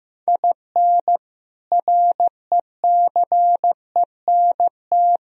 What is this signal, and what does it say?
Morse code, heard as a keyed tone, about 15 wpm, 700 Hz. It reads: IN RECENT
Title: Speed-Racing - Sets of 3 Words 15wpm